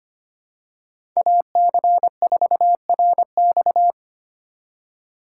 Morse code, heard as a keyed tone, about 25 wpm, 700 Hz. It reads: AC4RX
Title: USA Call Signs - 5 Characters - 3x Character Spacing - 25wpm